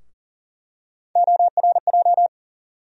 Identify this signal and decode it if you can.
Morse code, heard as a keyed tone, about 40 wpm, 700 Hz. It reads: ORJ
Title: USA Call Sign Postfixes - 3 Letters 40wpm